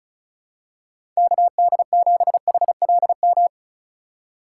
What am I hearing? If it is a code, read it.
Morse code, heard as a keyed tone, about 36 wpm, 700 Hz. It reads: KD7HLM
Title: USA Call Signs - 6 Characters - 4x Character Spacing - 35wpm